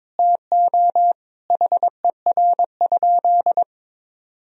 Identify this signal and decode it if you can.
Morse code, heard as a keyed tone, about 22 wpm, 700 Hz. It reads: TO HER?
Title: Sentences from Top 1000 Words - Review Part 1 - 22wpm